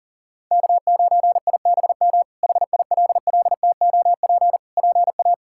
Morse code, heard as a keyed tone, about 40 wpm, 700 Hz. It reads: K9IBM HILLTOP PA
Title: K1USN SST QSO Practice - 40wpm